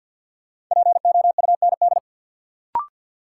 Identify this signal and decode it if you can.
Morse code, heard as a keyed tone, about 52 wpm, 700 Hz. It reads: POUND
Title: Sentences from Top 1000 Words - Review Part 2 - New Word Drill - 50wpm